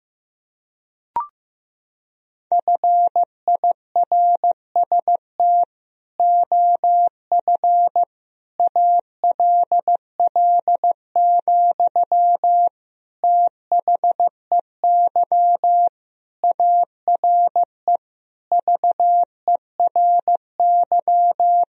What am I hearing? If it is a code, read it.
Morse code, heard as a keyed tone, about 16 wpm, 700 Hz. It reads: FIRST OF ALL, THEY ARE VERY
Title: Sentences from Top 1000 Words - Review Part 4 - 15wpm